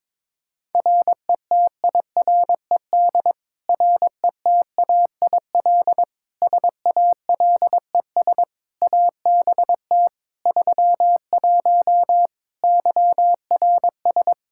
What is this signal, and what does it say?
Morse code, heard as a keyed tone, about 22 wpm, 700 Hz. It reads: RETIRED RETAIL SALES ABT 31 YRS
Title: Practice QSO - Exchange 3 - More Info - 22wpm